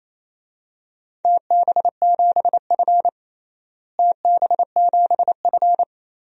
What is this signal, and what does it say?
Morse code, heard as a keyed tone, about 28 wpm, 700 Hz. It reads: TB7F TB7F
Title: DXCC-100 Call Signs - 4 Characters - Repeated 3x - 28wpm